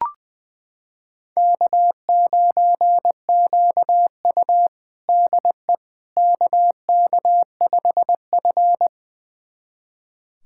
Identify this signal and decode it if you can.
Morse code, heard as a keyed tone, about 21 wpm, 700 Hz. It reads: K9QU DE KK5F
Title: USA Callsign DE Callsign - 4 Characters - 20wpm